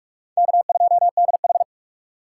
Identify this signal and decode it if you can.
Morse code, heard as a keyed tone, about 43 wpm, 700 Hz. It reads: K2DH